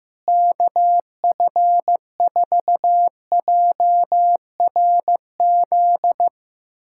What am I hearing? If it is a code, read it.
Morse code, heard as a keyed tone, about 15 wpm, 700 Hz. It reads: KF4JRZ
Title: USA Callsign DE Callsign 15wpm